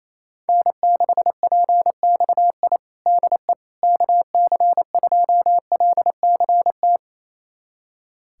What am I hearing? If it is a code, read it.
Morse code, heard as a keyed tone, about 28 wpm, 700 Hz. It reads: N6PXI DE KC2LCT
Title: USA Callsign DE Callsign 28wpm